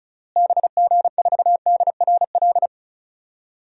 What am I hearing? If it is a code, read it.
Morse code, heard as a keyed tone, about 36 wpm, 700 Hz. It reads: BG4DRL
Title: DXCC-100 Call Signs - 6 Characters - Repeated 3x - 35wpm